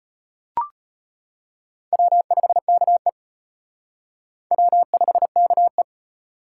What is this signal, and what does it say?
Morse code, heard as a keyed tone, about 31 wpm, 700 Hz. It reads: W5KE W5KE
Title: Speed Racing - USA 1x2 Call Signs 25wpm